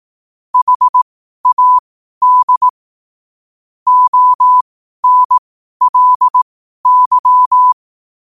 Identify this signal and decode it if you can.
Morse code, heard as a keyed tone, about 18 wpm, spaced as for 12 wpm, 1000 Hz. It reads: HAD ONLY
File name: qotdfiles / 12wpm